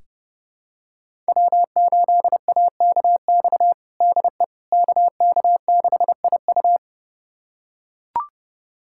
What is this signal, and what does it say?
Morse code, heard as a keyed tone, about 30 wpm, 700 Hz. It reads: W8AKX DE KK6IU
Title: USA Callsign DE Callsign - 5 Characters - 30wpm